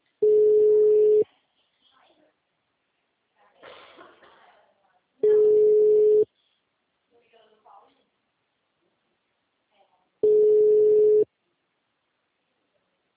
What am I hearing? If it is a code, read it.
Ringback tone (Europe, 425 Hz), with speech.
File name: out- -2003- - - .99